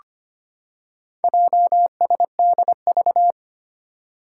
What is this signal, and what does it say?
Morse code, heard as a keyed tone, about 25 wpm, 700 Hz. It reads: JSDV
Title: Four Challenging Letter-Number Combinations 25wpm